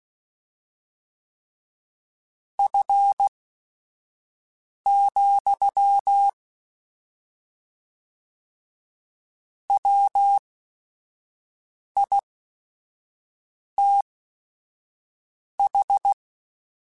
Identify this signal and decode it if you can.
Morse code, heard as a keyed tone, about 16 wpm, spaced as for 5 wpm, 800 Hz. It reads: F, WITH
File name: Weather Sets / Weather Sets-mp3s / Weather 4 21.@5